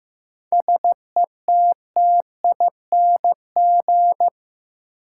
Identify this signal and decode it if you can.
Morse code, heard as a keyed tone, about 15 wpm, 700 Hz. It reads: SETTING
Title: Sentences from Top 1300 Words - New Word Drill Alt - 15wpm